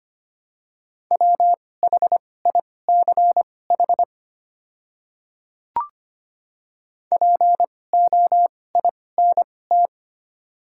Morse code, heard as a keyed tone, about 25 wpm, spaced as for 18 wpm, 690 Hz. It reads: WHICH POINT
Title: 2x Character Spacing - Top 200 Words - 25wpm